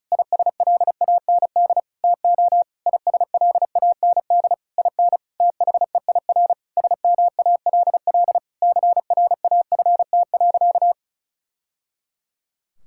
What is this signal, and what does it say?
Morse code, heard as a keyed tone, about 34 wpm, 700 Hz. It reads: ISLAND TO ISLAND IN THEIR SMALL CRAFT.
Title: Sentences from Top 1000 Words - Review Part 6 - 35wpm